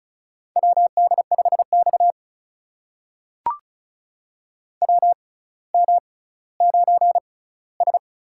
Morse code, heard as a keyed tone, about 36 wpm, spaced as for 18 wpm, 700 Hz. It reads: WD5X WM9S
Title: USA Call Signs - 4 Characters - 6x Character Spacing - 35wpm